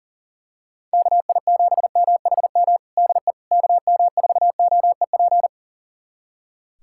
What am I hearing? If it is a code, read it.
Morse code, heard as a keyed tone, about 40 wpm, 690 Hz. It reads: KI7MHM DE KM4OEP
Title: USA Callsign DE Callsign - 6 Characters - 40wpm